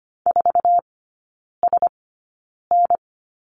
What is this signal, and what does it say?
Morse code, heard as a keyed tone, about 25 wpm, spaced as for 9 wpm, 700 Hz. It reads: 4SN
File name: course / 25:10wpm / 093 Lesson 16e - Letter c - Three Characters 25wpm